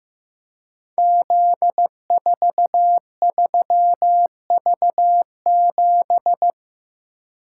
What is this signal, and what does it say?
Morse code, heard as a keyed tone, about 15 wpm, 700 Hz. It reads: Z43V7
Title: Code Groups - 5 Letter:Numbers - 15wpm